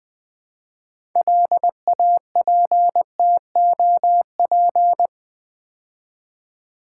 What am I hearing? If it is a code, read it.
Morse code, heard as a keyed tone, about 20 wpm, 700 Hz. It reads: LAPTOP
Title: Build A Word - 2 Letters At A Time - 6-Letter Words - 20wpm